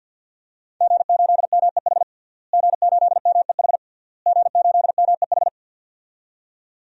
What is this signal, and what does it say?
Morse code, heard as a keyed tone, about 46 wpm, 700 Hz. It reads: G8MEH G8MEH G8MEH
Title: DXCC-100 Call Signs - 5 Characters - Repeated 3x - 50wpm